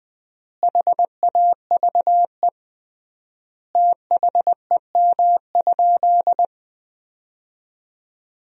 Morse code, heard as a keyed tone, about 21 wpm, 700 Hz. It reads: HAVE THEM?
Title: 3x Word-Spacing - Sentences from Top 100 Words - 20wpm